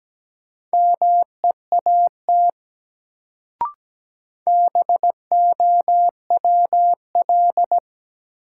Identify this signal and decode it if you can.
Morse code, heard as a keyed tone, about 17 wpm, 700 Hz. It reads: MEAT BOWL
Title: Sentences from Top 1200 Words - New Word Drill Alt - 17wpm